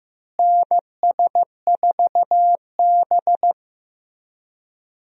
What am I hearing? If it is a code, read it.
Morse code, heard as a keyed tone, about 15 wpm, 700 Hz. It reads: NS4B
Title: USA Call Signs - 4 Characters - Repeated 3x - 15wpm